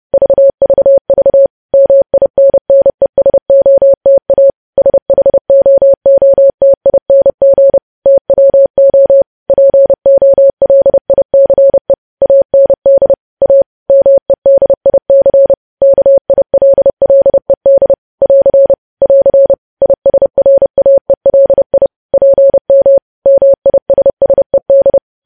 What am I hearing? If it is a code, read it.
Morse code, heard as a keyed tone, about 30 wpm, 550 Hz. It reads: VVV MINNESOTA SHOOTING TWO POLICE AND A MEDIC KILLED <AR> <AR> ISRAELI PM MISSED
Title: News Headlines at 30